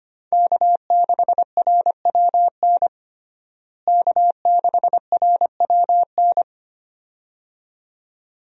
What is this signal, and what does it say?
Morse code, heard as a keyed tone, about 25 wpm, 700 Hz. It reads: K6RWN K6RWN
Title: USA Call Signs - 5 Characters - Repeated 2x - 25wpm